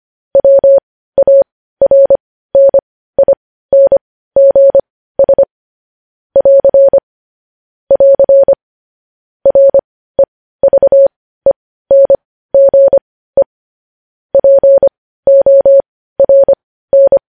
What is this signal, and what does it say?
Morse code, heard as a keyed tone, about 25 wpm, spaced as for 15 wpm, 550 Hz. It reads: WARNINGS <AR> <AR> REVENGE PORN
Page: news headlines in morse code at 15 wpm